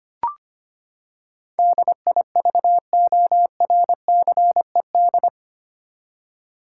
Morse code, heard as a keyed tone, about 25 wpm, 700 Hz. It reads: DIVORCED
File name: Eight Letter Words - 25wpm